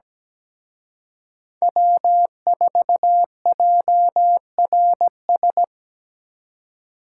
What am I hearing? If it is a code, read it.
Morse code, heard as a keyed tone, about 20 wpm, 700 Hz. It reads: W4JRS
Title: Speed Racing - USA 1x3 Call Signs 17wpm